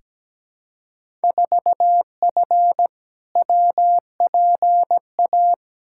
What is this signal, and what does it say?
Morse code, heard as a keyed tone, about 17 wpm, 700 Hz. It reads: 4F WPA
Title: Field Day Exchanges - 17wpm